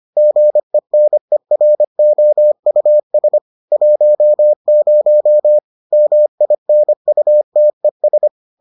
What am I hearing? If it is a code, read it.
Morse code, heard as a keyed tone, about 25 wpm, 600 Hz. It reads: GENEROUS 10 MINUTES